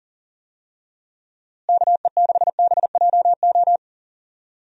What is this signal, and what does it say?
Morse code, heard as a keyed tone, about 40 wpm, 700 Hz. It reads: KE6BJO
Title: USA Call Signs - 6 Characters - Repeated 3x - 40wpm